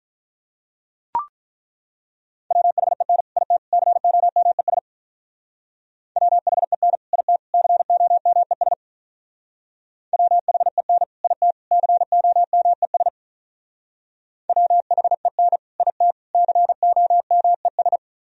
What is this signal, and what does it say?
Morse code, heard as a keyed tone, about 43 wpm, 700 Hz. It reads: WHEN IT COMES WHEN IT COMES WHEN IT COMES WHEN IT COMES
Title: Speed-Racing - Sets of 3 Words 35wpm